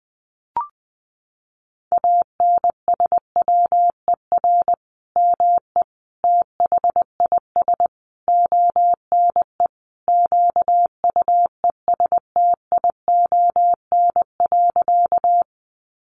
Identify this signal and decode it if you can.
Morse code, heard as a keyed tone, about 20 wpm, 700 Hz. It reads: ANSWER ME THIS ONE QUESTION.
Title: Sentences from Top 400 Words 20wpm